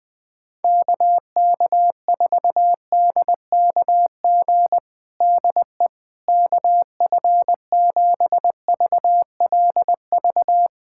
Morse code, heard as a keyed tone, about 21 wpm, 700 Hz. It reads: KK4DKG DE KF7VLV
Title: USA Callsign DE Callsign 20wpm